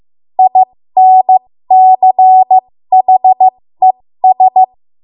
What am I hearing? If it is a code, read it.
Morse code, heard as a keyed tone, about 15 wpm, spaced as for 13 wpm, 750 Hz. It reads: INCHES
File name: Morse / 13WPM